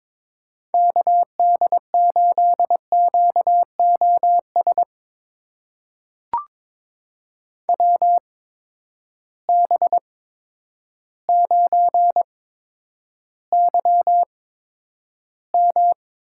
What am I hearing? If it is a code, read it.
Morse code, heard as a keyed tone, about 22 wpm, spaced as for 10 wpm, 700 Hz. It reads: KD8QOS WB9YM
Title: USA Call Signs - 6 Characters - 8x Character Spacing - 22wpm